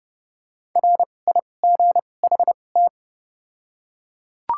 Morse code, heard as a keyed tone, about 30 wpm, spaced as for 22 wpm, 700 Hz. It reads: RIGHT
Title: 2x Character Spacing - Top 200 Words - 30wpm